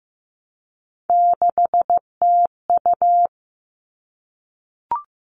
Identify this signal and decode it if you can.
Morse code, heard as a keyed tone, about 15 wpm, 700 Hz. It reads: 6TU
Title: All Cut Numbers - 3-Digit 15wpm